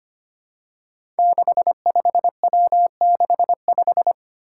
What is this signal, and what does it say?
Morse code, heard as a keyed tone, about 25 wpm, 700 Hz. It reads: 65W65
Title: Five Challenging Letter-Number Combinations 25wpm